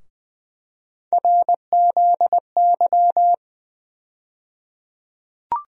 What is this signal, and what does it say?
Morse code, heard as a keyed tone, about 21 wpm, 700 Hz. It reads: RZY
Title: USA Call Sign Postfixes - 3 Letters 20wpm